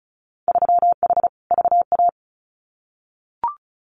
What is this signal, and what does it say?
Morse code, heard as a keyed tone, about 34 wpm, 700 Hz. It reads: 3H VA
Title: Winter Field Day Exchanges 35wpm